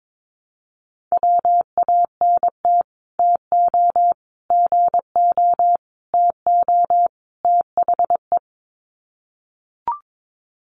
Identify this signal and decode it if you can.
Morse code, heard as a keyed tone, about 22 wpm, 700 Hz. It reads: WANT TO GO TO THE
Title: Sets of 5 Words Encore2 22wpm